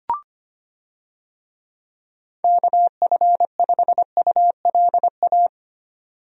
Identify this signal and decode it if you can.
Morse code, heard as a keyed tone, about 25 wpm, 700 Hz. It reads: KF5ULA
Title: USA Call Signs - 6 Characters - 25wpm